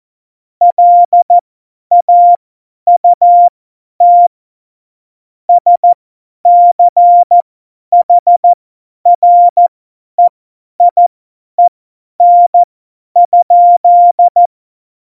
Morse code, heard as a keyed tone, about 14 wpm, spaced as for 10 wpm, 700 Hz. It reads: LAUT SCHREIEN?
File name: Morsekurs / Witze / Witz 08. September 2020, 50 F70